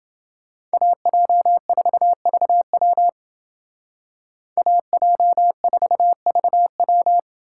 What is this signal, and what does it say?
Morse code, heard as a keyed tone, about 29 wpm, 700 Hz. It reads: AJ4VW AJ4VW